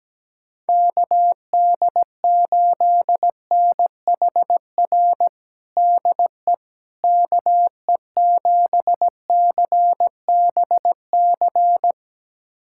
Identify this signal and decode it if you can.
Morse code, heard as a keyed tone, about 17 wpm, 700 Hz. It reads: KD8NHR DE KE7CBC
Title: USA Callsign DE Callsign 17wpm